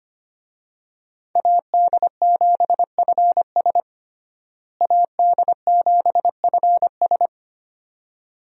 Morse code, heard as a keyed tone, about 25 wpm, 700 Hz. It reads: AD7FS AD7FS
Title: USA Call Signs - 5 Characters - Repeated 2x - 25wpm